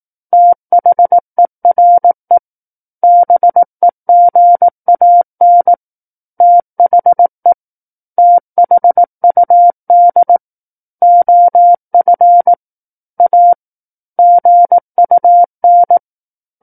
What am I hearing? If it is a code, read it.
Morse code, heard as a keyed tone, about 18 wpm, 700 Hz. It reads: THERE BEGAN THE THUD OF A GUN
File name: 18 WPM